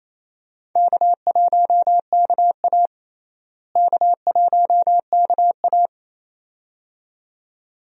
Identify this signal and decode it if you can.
Morse code, heard as a keyed tone, about 28 wpm, 700 Hz. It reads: K1KA K1KA
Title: USA Call Signs - 4 Characters - Repeated 2x - 28wpm